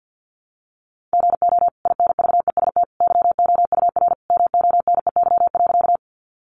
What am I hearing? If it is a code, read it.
Morse code, heard as a keyed tone, about 50 wpm, 700 Hz. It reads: GO INVEST YOUR MONEY.
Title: Sentences from Top 1100 Words 50wpm